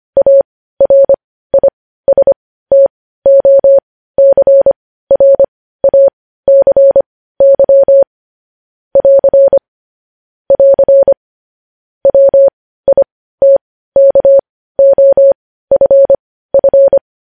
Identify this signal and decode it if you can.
Morse code, heard as a keyed tone, about 25 wpm, spaced as for 15 wpm, 550 Hz. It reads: ARISTOCRACY <AR> <AR> WITKOFF
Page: news-headlines-in-morse-code-at-15-wpm